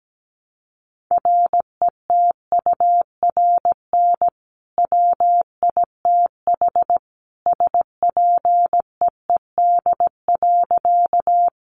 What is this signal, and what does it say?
Morse code, heard as a keyed tone, about 17 wpm, 700 Hz. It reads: RETURN WITH SPEED.
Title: Sentences from Top 800 Words 17wpm